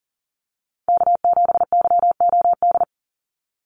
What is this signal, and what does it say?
Morse code, heard as a keyed tone, about 40 wpm, 700 Hz. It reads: K7YOD
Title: USA Call Signs 40wpm